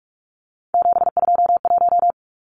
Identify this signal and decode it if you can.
Morse code, heard as a keyed tone, about 44 wpm, 700 Hz. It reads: 721
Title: Numbers - 3-Digit 45wpm